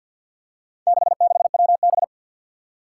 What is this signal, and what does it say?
Morse code, heard as a keyed tone, about 48 wpm, 700 Hz. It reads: 66WB
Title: Four Challenging Letter-Number Combinations 50wpm